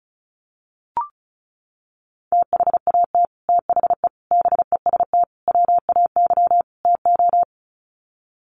Morse code, heard as a keyed tone, about 34 wpm, 700 Hz. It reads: THAT THE BEST WAY TO
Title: Sets of 5 Words Encore2 35wpm